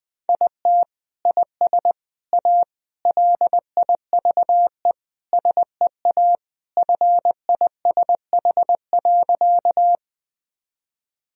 Morse code, heard as a keyed tone, about 20 wpm, 700 Hz. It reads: IT IS A LIVE SEA FISH.
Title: Sentences from Top 1000 Words - Review Part 1 - 20wpm